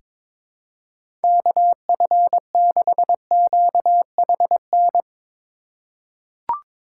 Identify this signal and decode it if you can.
Morse code, heard as a keyed tone, about 22 wpm, 700 Hz. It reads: KF6QHN
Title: USA Call Signs - 8x Character Spacing - 22wpm